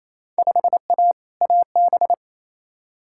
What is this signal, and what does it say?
Morse code, heard as a keyed tone, about 28 wpm, 700 Hz. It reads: 5A AB
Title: Field Day Exchanges Encore - 28wpm